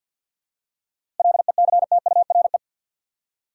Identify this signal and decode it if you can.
Morse code, heard as a keyed tone, about 50 wpm, 700 Hz. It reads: LECTURE
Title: Seven Letter Words - 50wpm